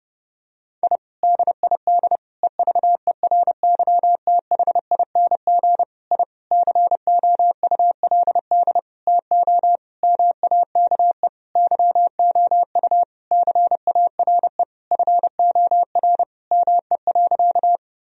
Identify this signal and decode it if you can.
Morse code, heard as a keyed tone, about 30 wpm, 700 Hz. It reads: I DID EVERYTHING I COULD TO MAKE YOU CARE FOR ME.
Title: Sentences from Top 1000 Words - Review Part 1 - 30wpm